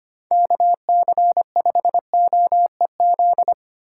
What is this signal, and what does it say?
Morse code, heard as a keyed tone, about 28 wpm, 700 Hz. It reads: KC5OEZ
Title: Speed Racing - USA 2x3 Call Signs 25wpm